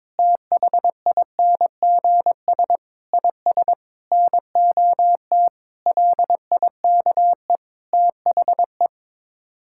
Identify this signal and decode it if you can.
Morse code, heard as a keyed tone, about 22 wpm, 700 Hz. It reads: THINGS IS NOT LIKE THE
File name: practice / 22wpm / Sets of 8 Words - 22wpm